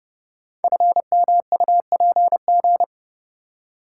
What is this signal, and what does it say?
Morse code, heard as a keyed tone, about 30 wpm, 700 Hz. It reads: FMUPG
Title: Code Groups - 5 Letters - 30wpm